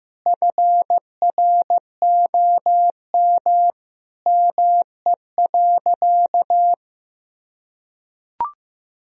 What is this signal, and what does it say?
Morse code, heard as a keyed tone, about 16 wpm, 700 Hz. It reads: FROM ME.
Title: Sentences from Top 1000 Words - Review Part 2 - 15wpm